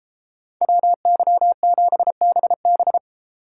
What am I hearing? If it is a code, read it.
Morse code, heard as a keyed tone, about 33 wpm, 700 Hz. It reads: WY7BB